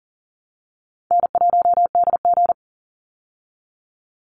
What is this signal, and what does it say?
Morse code, heard as a keyed tone, about 39 wpm, 700 Hz. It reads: N1DG
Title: Easy Call Signs Repeated 3x - 40wpm